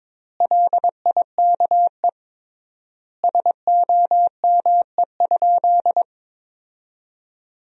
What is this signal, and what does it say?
Morse code, heard as a keyed tone, about 22 wpm, 700 Hz. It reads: LIKE SOME?
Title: 3x Word-Spacing - Sentences from Top 100 Words - 22wpm